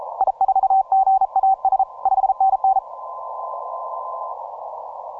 Morse code, heard as a keyed tone, about 34 wpm, 770 Hz. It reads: I4GAS 5NN
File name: simple-file-list / RI0B AS121 20MCW